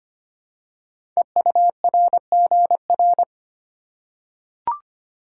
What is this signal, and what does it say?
Morse code, heard as a keyed tone, about 25 wpm, 700 Hz. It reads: EURGR
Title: Code Groups - 5 Letters - 25wpm